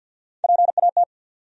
Morse code, heard as a keyed tone, about 53 wpm, 700 Hz. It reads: PAT
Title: USA Common Names - Max 3 Letters - 50wpm